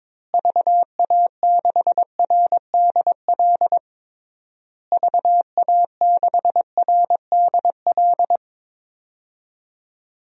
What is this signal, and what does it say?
Morse code, heard as a keyed tone, about 22 wpm, 700 Hz. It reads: VA6RDL VA6RDL
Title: DXCC-100 Call Signs - 6 Characters - Repeated 3x - 22wpm